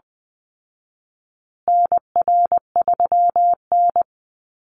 Morse code, heard as a keyed tone, about 20 wpm, 700 Hz. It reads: NR3N
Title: Call Signs Repeated 3x - 20wpm